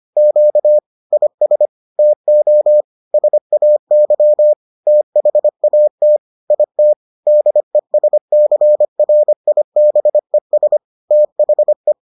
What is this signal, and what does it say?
Morse code, heard as a keyed tone, about 25 wpm, 600 Hz. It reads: Q IS TO SAY THAT IT DESCRIBES THE